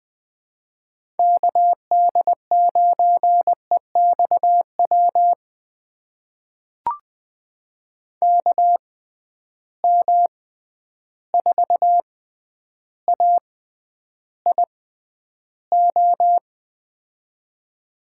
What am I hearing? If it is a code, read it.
Morse code, heard as a keyed tone, about 20 wpm, spaced as for 10 wpm, 700 Hz. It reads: KD9EXW KM4AIO
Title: USA Call Signs - 6 Characters - 6x Character Spacing - 20wpm